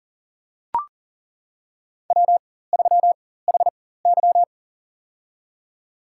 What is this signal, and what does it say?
Morse code, heard as a keyed tone, about 40 wpm, spaced as for 19 wpm, 700 Hz. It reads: W3HY
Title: USA Call Signs - 4 Characters - 4x Character Spacing - 40wpm